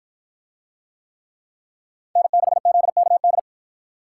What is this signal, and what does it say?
Morse code, heard as a keyed tone, about 52 wpm, 700 Hz. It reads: N6ZKD